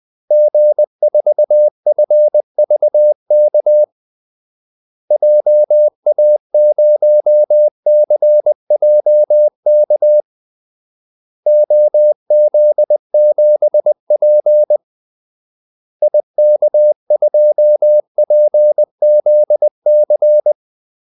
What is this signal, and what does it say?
Morse code, heard as a keyed tone, about 20 wpm, 600 Hz. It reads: G4FVK JA0CJK OZ7P IK2PZC
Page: Kaldesignaler 20 wpm